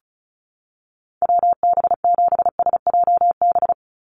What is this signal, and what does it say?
Morse code, heard as a keyed tone, about 34 wpm, 700 Hz. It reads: WB7SJB
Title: USA Call Signs 35wpm